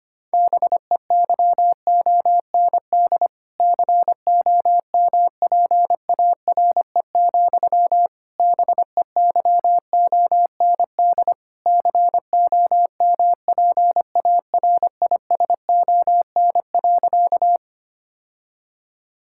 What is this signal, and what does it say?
Morse code, heard as a keyed tone, about 25 wpm, 700 Hz. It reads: BEYOND COMPARE, BEYOND COMPARISON.
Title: Sentences from Top 1600 Words 25wpm